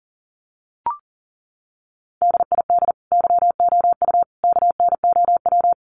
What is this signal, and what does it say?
Morse code, heard as a keyed tone, about 40 wpm, 700 Hz. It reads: DID YOU KNOW
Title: Sentences from Top 100 Words 40wpm